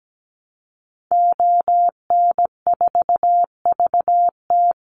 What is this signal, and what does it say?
Morse code, heard as a keyed tone, about 17 wpm, 700 Hz. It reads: ON4VT
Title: Call Signs Repeated 3x - 17wpm